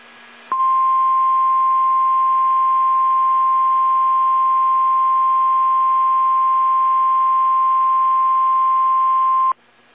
Emergency Alert System (SAME) tones (the 1050 Hz attention tone). It is heard from 0.5 to 9.6 s.
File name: eas / KSDZKDJL / KSDZKDJL